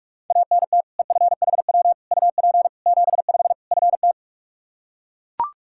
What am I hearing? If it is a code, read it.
Morse code, heard as a keyed tone, about 42 wpm, 700 Hz. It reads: ANT EFHW UP 75 FT
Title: Practice QSO - Exchange 2 - Info - 45wpm